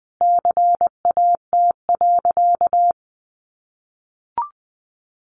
Morse code, heard as a keyed tone, about 20 wpm, 700 Hz. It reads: CAT.
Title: Easy Phrases 20wpm